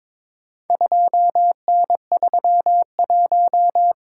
Morse code, heard as a keyed tone, about 22 wpm, 700 Hz. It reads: 2N31
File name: practice / 22wpm / Four Letter-Number Combinations - 22wpm